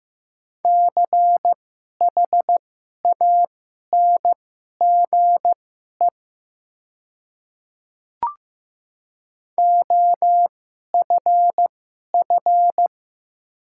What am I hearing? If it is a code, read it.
Morse code, heard as a keyed tone, about 15 wpm, spaced as for 11 wpm, 700 Hz. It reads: CHANGE OFF
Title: 2x Character Spacing - Top 300 Words - 15wpm